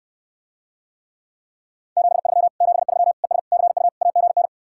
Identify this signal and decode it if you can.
Morse code, heard as a keyed tone, about 69 wpm, 700 Hz. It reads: 73 73 ES CU AGN